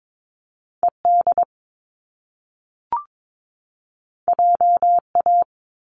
Morse code, heard as a keyed tone, about 22 wpm, 700 Hz. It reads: ED JA
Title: Top Two Letter Combinations in Words 22wpm